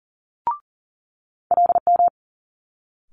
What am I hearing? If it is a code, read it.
Morse code, heard as a keyed tone, about 38 wpm, 700 Hz. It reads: LM
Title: Multi-letter Phonemes Quick Fire 40wpm